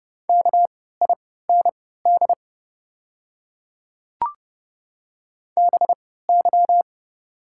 Morse code, heard as a keyed tone, about 30 wpm, spaced as for 17 wpm, 690 Hz. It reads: KIND BY
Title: 3x Character Spacing - Top 300 Words - 30wpm